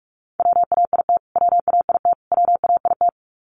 Morse code, heard as a keyed tone, about 44 wpm, 700 Hz. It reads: WAIT WAIT WAIT
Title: Top 500 Words Repeated 45wpm